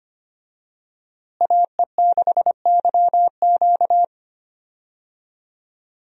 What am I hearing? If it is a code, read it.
Morse code, heard as a keyed tone, about 25 wpm, 700 Hz. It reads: AE6YQ
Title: USA Call Signs - 5 Characters - Repeated 3x - 25wpm